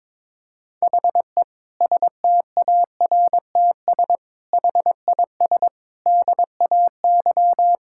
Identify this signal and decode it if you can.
Morse code, heard as a keyed tone, about 22 wpm, 700 Hz. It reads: HE STARTS HIS DAY
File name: Difficult Sentences with Hints - 22wpm